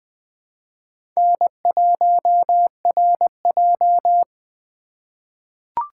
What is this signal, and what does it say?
Morse code, heard as a keyed tone, about 20 wpm, 700 Hz. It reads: N1RJ
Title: USA Call Signs - 4 Characters - 3x Character Spacing - 20wpm